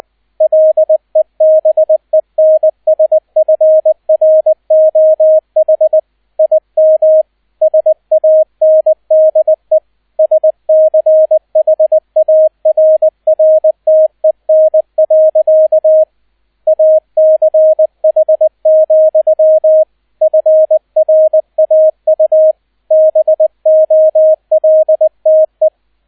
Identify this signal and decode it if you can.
Morse code, heard as a keyed tone, about 20 wpm, 630 Hz, with radio noise. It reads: LEBENSFROH IM SANDE SCHARRTEN. ACH, FRAU BOLTE
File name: 002 Zweiter Streich 20wpm